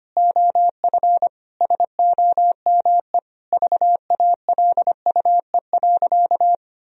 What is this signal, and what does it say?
Morse code, heard as a keyed tone, about 25 wpm, 700 Hz. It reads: OF SOME VALUE.
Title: Sentences from Top 1000 Words - Review Part 3 - 25wpm